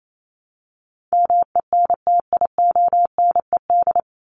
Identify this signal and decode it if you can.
Morse code, heard as a keyed tone, about 28 wpm, 700 Hz. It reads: MENTIONED
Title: ED Suffix - Top 100 Words with Base Words 28wpm